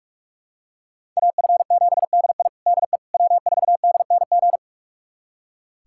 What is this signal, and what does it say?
Morse code, heard as a keyed tone, about 45 wpm, 700 Hz. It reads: AF7DI DE W4DNG
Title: USA Callsign DE Callsign - 5 Characters - 45wpm